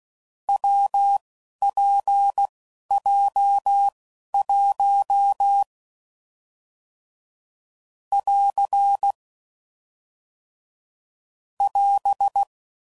Morse code, heard as a keyed tone, about 16 wpm, spaced as for 11 wpm, 800 Hz. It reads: WPJ1 <AR> <AS>
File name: Combinations / Difficult Combos / WPJ WPJ1